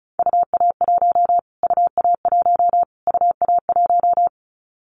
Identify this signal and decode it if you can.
Morse code, heard as a keyed tone, about 34 wpm, 700 Hz. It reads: UA1 UA1 UA1
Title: Callsign Prefixes for Top 25 DXCC - Repeated 3x - 35wpm